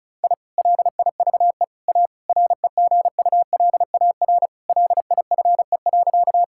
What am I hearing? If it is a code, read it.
Morse code, heard as a keyed tone, about 36 wpm, 700 Hz. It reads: I LIVE A REGULAR LIFE.
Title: Sentences from Top 1000 Words - Review Part 6 - 35wpm